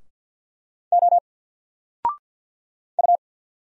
Morse code, heard as a keyed tone, about 50 wpm, 700 Hz. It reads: O U
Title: Single Letter-Number Alt 50wpm